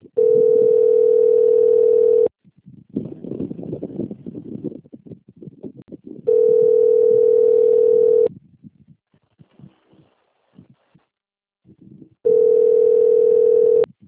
Ringback tone (North America), 440 + 480 Hz.
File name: out- - - - - .71813